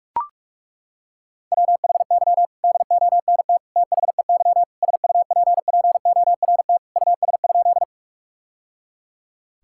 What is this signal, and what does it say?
Morse code, heard as a keyed tone, about 41 wpm, 700 Hz. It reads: WHY DONT THEY SUPPORT US?
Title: Sentences from Top 1000 Words - Review Part 5 - 45wpm